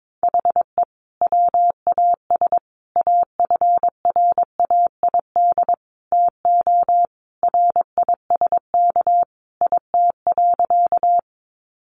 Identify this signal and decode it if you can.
Morse code, heard as a keyed tone, about 22 wpm, 700 Hz. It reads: HE WAS AFRAID TO RISK IT.
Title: Sentences from Top 1500 Words 22wpm